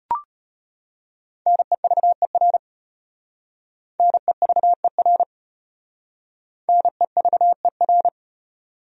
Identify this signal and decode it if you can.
Morse code, heard as a keyed tone, about 33 wpm, 700 Hz. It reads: NEVER NEVER NEVER